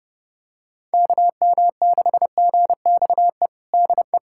Morse code, heard as a keyed tone, about 29 wpm, 700 Hz. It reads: KM6GXE DE
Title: USA Callsign DE Callsign - 6 Characters - 30wpm